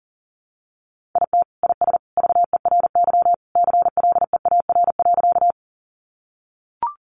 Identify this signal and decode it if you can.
Morse code, heard as a keyed tone, about 40 wpm, 700 Hz. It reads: IT IS VERY CLEAR.
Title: Sentences from Top 500 Words 40wpm